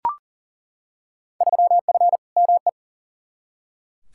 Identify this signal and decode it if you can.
Morse code, heard as a keyed tone, about 40 wpm, 700 Hz. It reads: 3F ME
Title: Field Day Exchanges - 40wpm